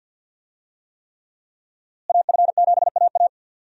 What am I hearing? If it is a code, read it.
Morse code, heard as a keyed tone, about 52 wpm, 700 Hz. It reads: AF7AA
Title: USA Call Signs - 5 Characters - Repeated 3x - 50wpm